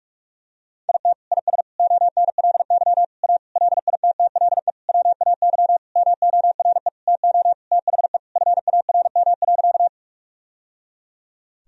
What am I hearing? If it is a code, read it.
Morse code, heard as a keyed tone, about 46 wpm, 700 Hz. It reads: IT IS ONLY A LITTLE WAY MORE TO THE FARM.
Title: Sentences from Top 1000 Words - Review Part 2 - 45wpm